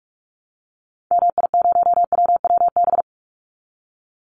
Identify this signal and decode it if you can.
Morse code, heard as a keyed tone, about 44 wpm, 700 Hz. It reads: MI0WWB
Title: Call Signs Repeated 2x - 45wpm